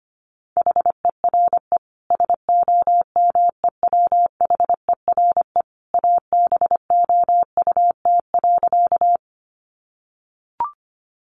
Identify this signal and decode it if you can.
Morse code, heard as a keyed tone, about 25 wpm, 700 Hz. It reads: HERE SOMEWHERE ABOUT.
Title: Sentences from Top 1400 Words 25wpm